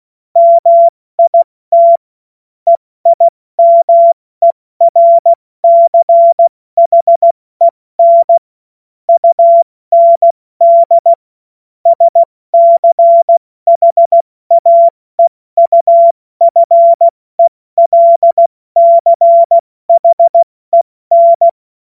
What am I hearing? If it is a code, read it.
Morse code, heard as a keyed tone, about 16 wpm, 680 Hz. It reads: MIT EIMERCHEN UND SCHAEUFELCHEN
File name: Morsekurs / Witze / Witz 28.November 2023 70 F80